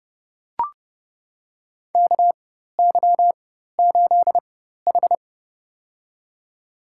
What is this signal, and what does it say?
Morse code, heard as a keyed tone, about 30 wpm, spaced as for 14 wpm, 700 Hz. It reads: KY8H
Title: USA Call Signs - 4x Character Spacing - 30wpm